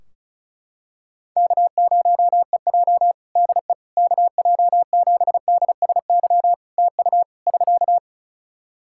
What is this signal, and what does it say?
Morse code, heard as a keyed tone, about 34 wpm, 700 Hz. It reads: K0EJ DE KJ7DSY TU <SK>
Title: Practice QSO - Exchange 4 - Ending - 35wpm